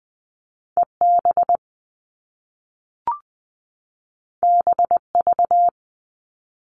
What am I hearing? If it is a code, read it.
Morse code, heard as a keyed tone, about 20 wpm, 700 Hz. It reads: EB BV
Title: All Cut Numbers - 2-Digit 20wpm